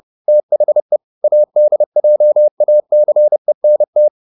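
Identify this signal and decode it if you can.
Morse code, heard as a keyed tone, about 30 wpm, 600 Hz. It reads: THE ADJACENT
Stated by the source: Morse code at 30 wpm, the novel Never by Ken Follett in English, from CW med Gnister.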